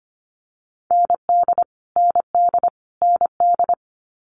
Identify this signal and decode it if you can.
Morse code, heard as a keyed tone, about 25 wpm, 700 Hz. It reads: ND ND ND
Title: Multi-letter Phonemes Repeated 3x 25wpm